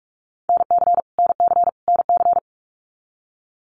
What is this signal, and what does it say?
Morse code, heard as a keyed tone, about 44 wpm, 700 Hz. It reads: NC NC NC
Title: US States Repeated 3x 45wpm